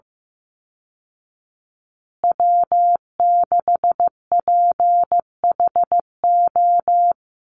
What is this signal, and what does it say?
Morse code, heard as a keyed tone, about 15 wpm, 700 Hz. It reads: W6PHO
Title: Call Signs Repeated 3x - 15wpm